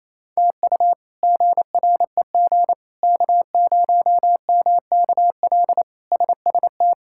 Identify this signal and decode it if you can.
Morse code, heard as a keyed tone, about 28 wpm, 700 Hz. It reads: TU GREG K0MKL SST
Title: K1USN SST QSO Practice - 28wpm